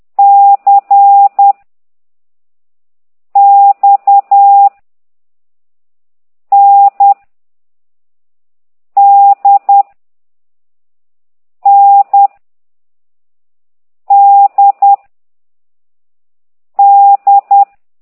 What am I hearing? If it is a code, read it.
Morse code, heard as a keyed tone, about 10 wpm, 800 Hz. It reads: C X N D N D D
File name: ct2idw / exerc4